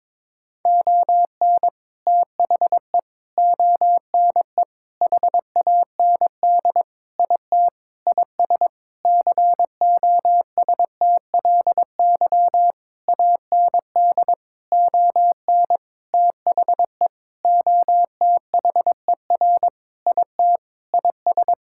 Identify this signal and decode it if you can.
Morse code, heard as a keyed tone, about 22 wpm, 700 Hz. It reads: ON THE ONE HAND IT IS COSTLY AND ON THE OTHER IT IS
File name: practice / 22wpm / Sentences with 8-Word Phrases - 22wpm